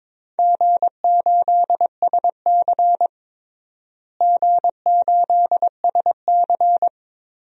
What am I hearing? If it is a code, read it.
Morse code, heard as a keyed tone, about 21 wpm, 700 Hz. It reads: G8SC G8SC
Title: DXCC-100 Call Signs - 4 Characters - Repeated 3x - 22wpm